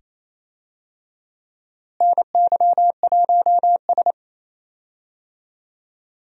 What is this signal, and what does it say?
Morse code, heard as a keyed tone, about 28 wpm, 700 Hz. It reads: NY1S
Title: USA Call Signs - 4 Characters - Repeated 2x - 28wpm